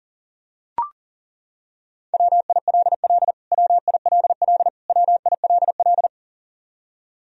Morse code, heard as a keyed tone, about 40 wpm, 700 Hz. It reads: WILL WILL WILL
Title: Top 100 Words Repeated 40wpm